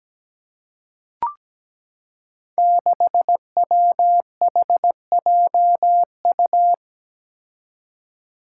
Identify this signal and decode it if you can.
Morse code, heard as a keyed tone, about 17 wpm, 700 Hz. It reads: 6WHJU
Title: Five Challenging Letter-Number Combinations 17wpm